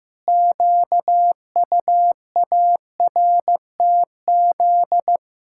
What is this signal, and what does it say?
Morse code, heard as a keyed tone, about 16 wpm, 700 Hz. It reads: QUARTZ
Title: Pangram Challenge v2 - 15wpm